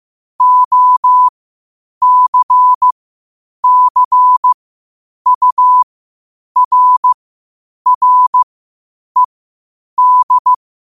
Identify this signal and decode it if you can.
Morse code, heard as a keyed tone, about 15 wpm, spaced as for 8 wpm, 1000 Hz. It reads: OCCURRED
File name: qotdfiles / 8wpm